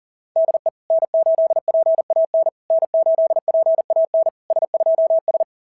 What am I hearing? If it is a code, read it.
Morse code, heard as a keyed tone, about 40 wpm, 690 Hz. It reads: DE N8PAN N8PAN S2S
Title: SOTA QSO Practice 40wpm